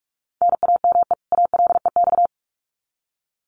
Morse code, heard as a keyed tone, about 44 wpm, 700 Hz. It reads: NAME ALEX
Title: Common QSO Phrases 45wpm